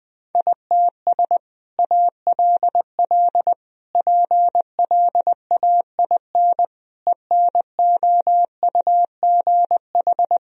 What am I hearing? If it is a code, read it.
Morse code, heard as a keyed tone, about 20 wpm, 700 Hz. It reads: ITS ALL PLAIN ENOUGH
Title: Sentences from Top 1000 Words - Review Part 2 - 20wpm